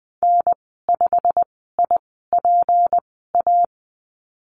Morse code, heard as a keyed tone, about 20 wpm, spaced as for 15 wpm, 700 Hz. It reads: N5IPA
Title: 179 Lesson 29g - Slash - Call Signs 20wpm